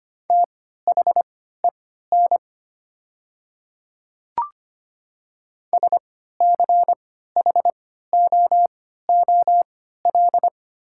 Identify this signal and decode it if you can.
Morse code, heard as a keyed tone, about 25 wpm, spaced as for 14 wpm, 700 Hz. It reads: THEN SCHOOL
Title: 3x Character Spacing - Top 300 Words - 25wpm